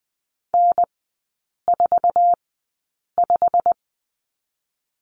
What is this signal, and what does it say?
Morse code, heard as a keyed tone, about 20 wpm, spaced as for 8 wpm, 700 Hz. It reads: N45
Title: 087 Lesson 15e - Number 5 - Three Characters 20wpm